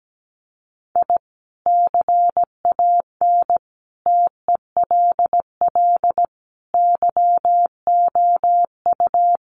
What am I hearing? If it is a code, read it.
Morse code, heard as a keyed tone, about 17 wpm, 700 Hz. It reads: I CAN TELL YOU
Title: Sets of 5 Words 17wpm